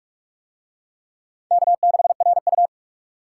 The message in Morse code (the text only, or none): K6RU